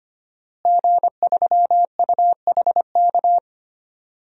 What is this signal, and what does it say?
Morse code, heard as a keyed tone, about 25 wpm, 700 Hz. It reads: G3UHK
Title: DXCC-100 Call Signs - 5 Characters - Repeated 3x - 25wpm